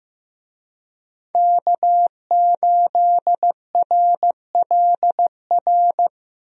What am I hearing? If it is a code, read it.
Morse code, heard as a keyed tone, about 16 wpm, 700 Hz. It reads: K8RLR
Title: USA Call Signs - 5 Characters - Repeated 3x - 15wpm